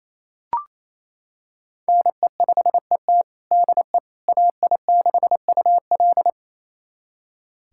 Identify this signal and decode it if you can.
Morse code, heard as a keyed tone, about 28 wpm, 700 Hz. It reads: NE5ET DE AI6UL
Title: USA Callsign DE Callsign - 5 Characters - 28wpm